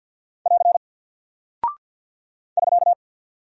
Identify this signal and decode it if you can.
Morse code, heard as a keyed tone, about 48 wpm, 700 Hz. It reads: <AR> <SK>
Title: Single Character (Letter:Number:Punctuation:Prosign) Alt 50wpm